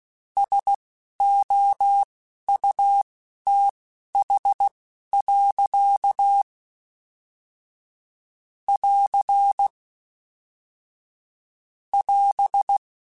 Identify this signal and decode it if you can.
Morse code, heard as a keyed tone, about 16 wpm, spaced as for 11 wpm, 800 Hz. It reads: SOUTH. <AR> <AS>
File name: Weather Sets / Weather Sets-mp3s / Weather 4 21 @10